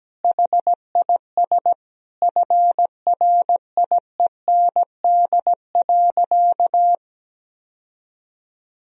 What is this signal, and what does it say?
Morse code, heard as a keyed tone, about 17 wpm, 700 Hz. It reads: HIS FRIEND.
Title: Sentences from Top 1000 Words - Review Part 4 - 17wpm